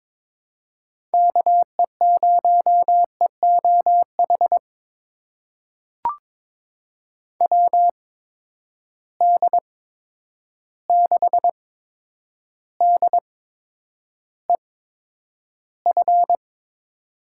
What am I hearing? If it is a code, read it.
Morse code, heard as a keyed tone, about 22 wpm, spaced as for 9 wpm, 700 Hz. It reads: KE0EOH WD6DEF
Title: USA Call Signs - 6 Characters - 8x Character Spacing - 22wpm